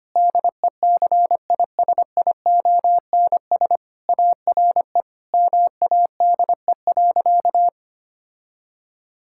Morse code, heard as a keyed tone, about 25 wpm, 700 Hz. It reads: DECISIONS ARE MADE.
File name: practice / 25wpm / Sentences with 4-Word Phrases - 25wpm